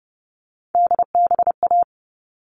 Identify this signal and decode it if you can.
Morse code, heard as a keyed tone, about 30 wpm, 700 Hz. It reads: DBA
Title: All Cut Numbers - 3-Digit 30wpm